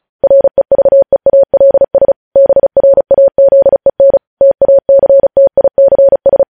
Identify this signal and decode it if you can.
Morse code, heard as a keyed tone, about 36 wpm, 550 Hz. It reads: REVEALS BRAZEN TACTICS
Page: News headlines in Morse code at 35 wpm